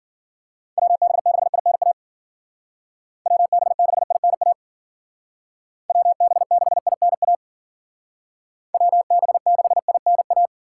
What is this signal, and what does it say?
Morse code, heard as a keyed tone, about 52 wpm, 700 Hz. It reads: WB6INA WB6INA WB6INA WB6INA